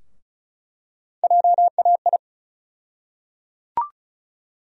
Morse code, heard as a keyed tone, about 34 wpm, 700 Hz. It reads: JAI
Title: USA Call Sign Postfixes - 3 Letters 35wpm